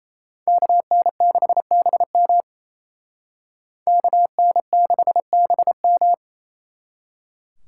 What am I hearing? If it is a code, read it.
Morse code, heard as a keyed tone, about 35 wpm, 700 Hz. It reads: KN6BM KN6BM
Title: Speed Racing - USA 2x2 Call Signs 28wpm